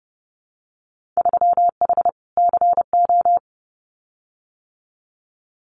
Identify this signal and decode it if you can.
Morse code, heard as a keyed tone, about 30 wpm, 700 Hz. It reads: 3H CO
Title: Winter Field Day Exchanges 30wpm